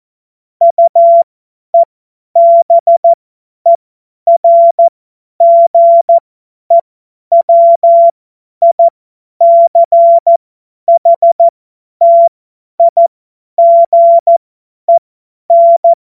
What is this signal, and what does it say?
Morse code, heard as a keyed tone, about 14 wpm, spaced as for 10 wpm, 680 Hz. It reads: UEBERGEWICHTIGEN
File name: Morsekurs / Witze / Witz 19 März 2024 50 F70